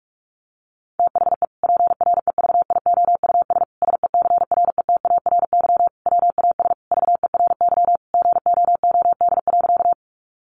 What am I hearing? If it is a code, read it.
Morse code, heard as a keyed tone, about 44 wpm, 700 Hz. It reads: THE PREVIOUS SECRETARY WAS VERY GOOD.
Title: Sentences from Top 1500 Words 45wpm